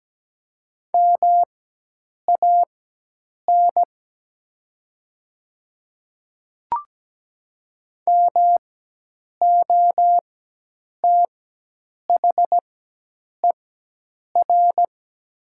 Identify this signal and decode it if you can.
Morse code, heard as a keyed tone, about 17 wpm, spaced as for 8 wpm, 700 Hz. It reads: MAN MOTHER
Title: 4x Character Spacing - Top 200 Words - 17wpm